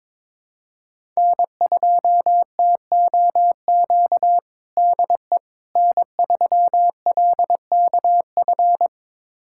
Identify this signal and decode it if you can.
Morse code, heard as a keyed tone, about 22 wpm, 700 Hz. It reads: N2TOQ DE N3LKF
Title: USA Callsign DE Callsign - 5 Characters - 22wpm